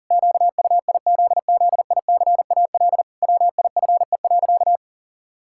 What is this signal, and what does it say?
Morse code, heard as a keyed tone, about 37 wpm, 700 Hz. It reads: QUIZZICAL WIFE.
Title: Pangram Challenge v2 - 40wpm